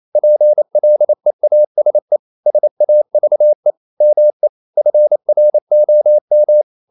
Morse code, heard as a keyed tone, about 28 wpm, 600 Hz. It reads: PLEASE SAVE ME FROM
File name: CW / Practice / EnglishQuotations0228